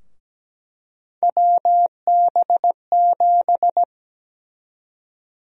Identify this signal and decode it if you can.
Morse code, heard as a keyed tone, about 17 wpm, 700 Hz. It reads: WB7
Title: USA Call Sign Prefixes - 17wpm